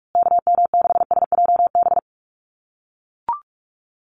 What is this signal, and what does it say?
Morse code, heard as a keyed tone, about 44 wpm, 700 Hz. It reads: KM6SJB
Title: USA Call Signs Repeated 3x 45wpm